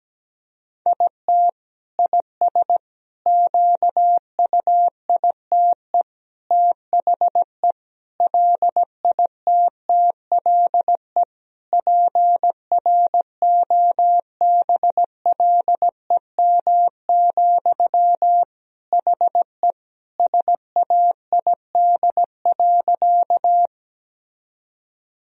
Morse code, heard as a keyed tone, about 17 wpm, 700 Hz. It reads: IT IS QUITE THE LITTLE PROBLEM, HE SAID.
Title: Sentences from Top 1000 Words - Review Part 2 - 17wpm